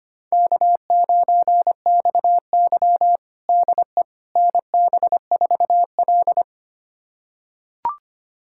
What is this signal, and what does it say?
Morse code, heard as a keyed tone, about 25 wpm, 700 Hz. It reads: K9XY DE NB4L
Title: USA Callsign DE Callsign - 4 Characters - 25wpm